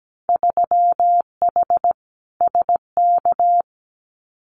Morse code, heard as a keyed tone, about 17 wpm, 700 Hz. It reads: 3H SK
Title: Winter Field Day Exchanges 17wpm